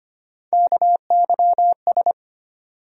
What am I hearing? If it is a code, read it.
Morse code, heard as a keyed tone, about 25 wpm, 700 Hz. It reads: KYS
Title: USA Call Sign Postfixes - 3 Letters 25wpm